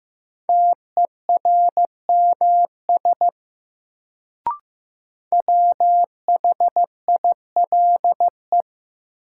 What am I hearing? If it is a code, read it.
Morse code, heard as a keyed tone, about 15 wpm, 700 Hz. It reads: TERMS WHILE
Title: Sets of 3 Words - New Word Drill - 15wpm